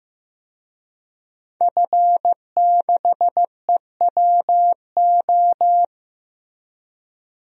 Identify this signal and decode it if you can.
Morse code, heard as a keyed tone, about 16 wpm, 700 Hz. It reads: F6EWO
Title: DXCC-100 Call Signs - 5 Characters - Repeated 3x - 15wpm